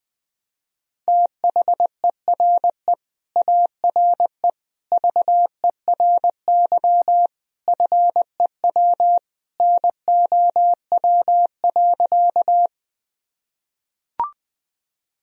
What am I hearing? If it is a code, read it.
Morse code, heard as a keyed tone, about 21 wpm, 700 Hz. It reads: THERE ARE VERY FEW NOW.
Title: Sentences from Top 1000 Words - Review Part 1 - 20wpm